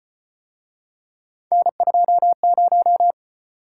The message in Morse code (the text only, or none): N20